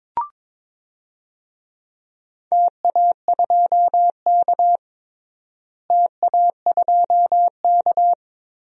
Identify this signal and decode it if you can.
Morse code, heard as a keyed tone, about 22 wpm, 700 Hz. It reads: TA2K TA2K
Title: DXCC-100 Call Signs - 4 Characters - Repeated 2x - 22wpm